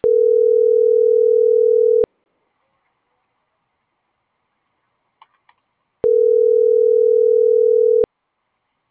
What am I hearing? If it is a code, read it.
Ringback tone (North America), 440 + 480 Hz.